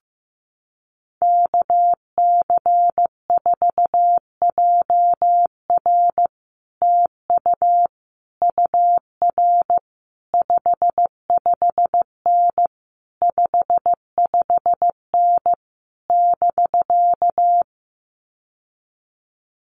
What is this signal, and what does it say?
Morse code, heard as a keyed tone, about 15 wpm, 700 Hz. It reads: KC4JR TU UR 55N 55N <BK>
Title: Common QSO Phrases Encore 15wpm